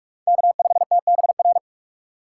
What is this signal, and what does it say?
Morse code, heard as a keyed tone, about 42 wpm, 700 Hz. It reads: K5TBR
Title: USA Call Signs - 5 Characters - 2x Character Spacing - 45wpm